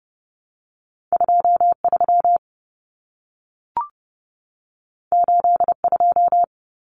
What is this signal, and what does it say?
Morse code, heard as a keyed tone, about 30 wpm, 700 Hz. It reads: 23 82
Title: Numbers - 2-Digit 30wpm